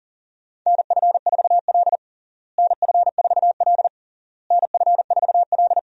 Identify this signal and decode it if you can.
Morse code, heard as a keyed tone, about 37 wpm, 700 Hz. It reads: NF4L NF4L NF4L
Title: USA Call Signs - 4 Characters - Repeated 3x - 40wpm